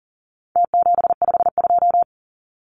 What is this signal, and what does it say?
Morse code, heard as a keyed tone, about 38 wpm, 700 Hz. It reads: T752
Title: Numbers - Contest Style - 4-Digit 40wpm